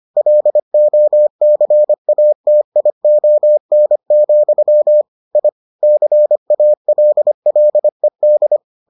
Morse code, heard as a keyed tone, about 25 wpm, 600 Hz. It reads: LOCATION, I CALLED